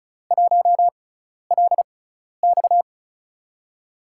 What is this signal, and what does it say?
Morse code, heard as a keyed tone, about 36 wpm, spaced as for 12 wpm, 700 Hz. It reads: 1LX
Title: USA Call Signs - 5 Characters - 6x Character Spacing - 35wpm